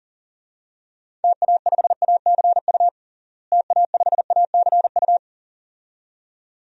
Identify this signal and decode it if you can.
Morse code, heard as a keyed tone, about 40 wpm, 700 Hz. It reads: TA5ACU TA5ACU
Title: DXCC-100 Call Signs - 6 Characters - Repeated 2x - 40wpm